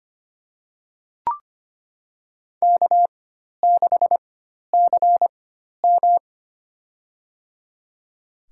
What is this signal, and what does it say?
Morse code, heard as a keyed tone, about 25 wpm, spaced as for 12 wpm, 700 Hz. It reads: K6CM
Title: USA Call Signs - 4x Character Spacing - 25wpm